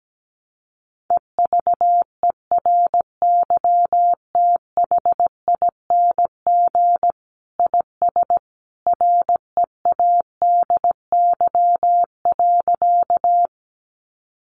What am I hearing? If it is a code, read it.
Morse code, heard as a keyed tone, about 17 wpm, 700 Hz. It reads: EVERYTHING IS READY.
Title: Sentences from Top 700 Words 17wpm